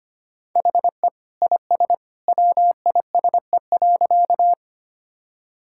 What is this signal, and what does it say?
Morse code, heard as a keyed tone, about 25 wpm, 700 Hz. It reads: HE IS WISE.
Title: Sentences from Top 1600 Words 25wpm